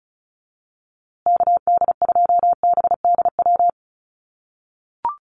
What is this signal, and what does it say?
Morse code, heard as a keyed tone, about 34 wpm, 700 Hz. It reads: KD2BDW
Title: USA Call Signs Repeated 3x 35wpm